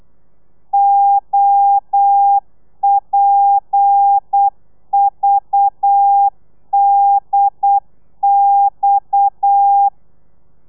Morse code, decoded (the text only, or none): OPVDX